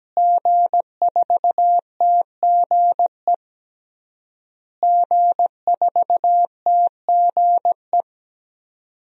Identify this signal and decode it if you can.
Morse code, heard as a keyed tone, about 17 wpm, 700 Hz. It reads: G4TGE G4TGE
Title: DXCC-100 Call Signs - 5 Characters - Repeated 3x - 17wpm